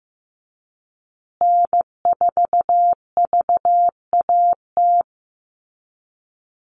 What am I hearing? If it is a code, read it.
Morse code, heard as a keyed tone, about 15 wpm, 700 Hz. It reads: N4VAT
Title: USA Call Signs Repeated 3x 15wpm